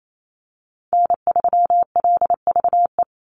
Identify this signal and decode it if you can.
Morse code, heard as a keyed tone, about 28 wpm, 700 Hz. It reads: N3LVE
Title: USA Call Signs 28wpm